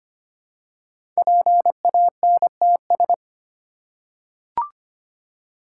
Morse code, heard as a keyed tone, about 25 wpm, 700 Hz. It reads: PANTS
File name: 5 Letter Words - First 3 Letters Given - 25wpm